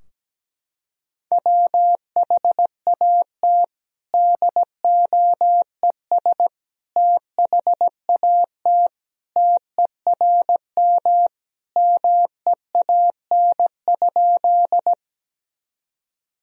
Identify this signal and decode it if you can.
Morse code, heard as a keyed tone, about 17 wpm, 700 Hz. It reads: WHAT DOES THAT TERM MEAN?
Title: Sentences from Top 1000 Words - Review Part 4 - 17wpm